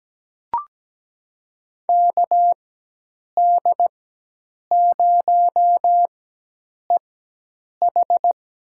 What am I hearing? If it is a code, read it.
Morse code, heard as a keyed tone, about 17 wpm, spaced as for 8 wpm, 700 Hz. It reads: KD0EH
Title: USA Call Signs - 5 Characters - 4x Character Spacing - 17wpm